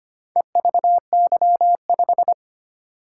Morse code, heard as a keyed tone, about 25 wpm, 700 Hz. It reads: EVY5
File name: practice / 25wpm / Four Letter-Number Combinations - 25wpm